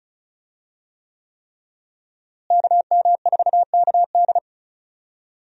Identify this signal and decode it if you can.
Morse code, heard as a keyed tone, about 34 wpm, 700 Hz. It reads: KM4KD
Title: USA Call Signs - 5 Characters - Repeated 3x - 35wpm